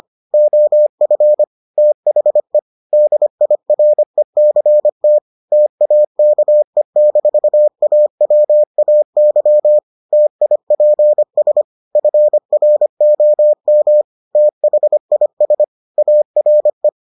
Morse code, heard as a keyed tone, about 25 wpm, 600 Hz. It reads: OF THE DIRECT TAKE-AWAY TIPS FROM THIS ARE